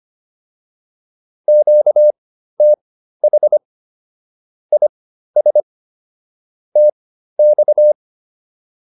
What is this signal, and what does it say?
Morse code, heard as a keyed tone, about 25 wpm, spaced as for 13 wpm, 600 Hz. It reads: QTH IS TX